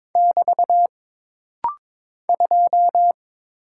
Morse code, heard as a keyed tone, about 22 wpm, 700 Hz. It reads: <BT> 2
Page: This rapid-fire timing is geared to push your ICR (Instant Character Recognition) skills.